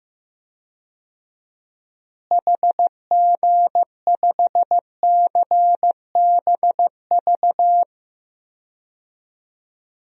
Morse code, heard as a keyed tone, about 15 wpm, 700 Hz. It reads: HG5CBV
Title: DXCC-100 Call Signs - 6 Characters - Repeated 2x - 15wpm